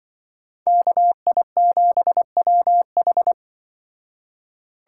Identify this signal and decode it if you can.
Morse code, heard as a keyed tone, about 25 wpm, 700 Hz. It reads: KI7WH